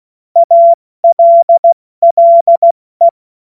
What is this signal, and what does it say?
Morse code, heard as a keyed tone, about 16 wpm, spaced as for 14 wpm, 680 Hz. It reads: ALLE
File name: Morsekurs / Witze / Witz 02. Februar 2021, 70 F80